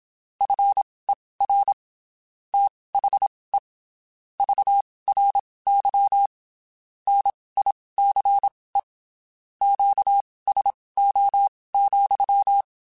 Morse code, heard as a keyed tone, about 27 wpm, spaced as for 19 wpm, 800 Hz. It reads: FER THE VRY NICE QSO,
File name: (spaced) / QSOs Spaced-mp3s / kc7sac @25 sp